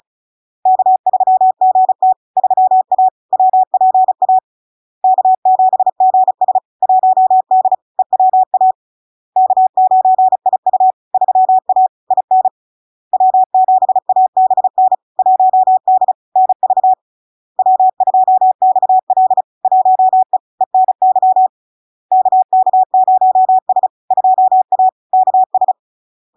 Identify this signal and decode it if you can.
Morse code, heard as a keyed tone, about 34 wpm, 750 Hz. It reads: K3GT 3A WPA K7GS 1D EWA K9IU 3A IN W7ABN 1D NV W2XL 1E ENY KK0S 2A KS
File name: LARK-FD-2022-Log-in-CW-35wpm-750-Hz